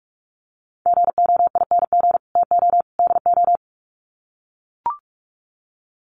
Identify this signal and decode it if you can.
Morse code, heard as a keyed tone, about 44 wpm, 700 Hz. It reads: GOING TO DO
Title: Sets of 3 Words 45wpm